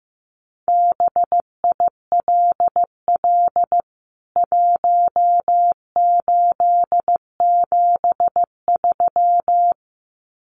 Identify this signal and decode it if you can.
Morse code, heard as a keyed tone, about 15 wpm, 700 Hz. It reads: BILL 1873
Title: CWT Exchange 15wpm